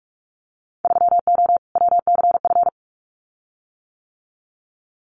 Morse code, heard as a keyed tone, about 46 wpm, 700 Hz. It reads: 3O WCF
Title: Winter Field Day Exchanges 45wpm